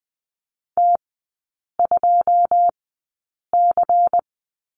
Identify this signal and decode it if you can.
Morse code, heard as a keyed tone, about 19 wpm, spaced as for 8 wpm, 700 Hz. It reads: T2C
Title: 121 Lesson 20e - Number 3 - Three Characters 20wpm